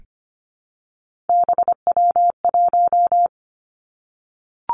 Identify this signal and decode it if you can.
Morse code, heard as a keyed tone, about 25 wpm, 700 Hz. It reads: BW1
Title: Three Challenging Letter-Number Combinations 25wpm